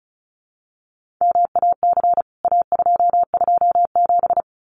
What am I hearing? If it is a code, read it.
Morse code, heard as a keyed tone, about 34 wpm, 700 Hz. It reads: MAC A227
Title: CWT Exchange 35wpm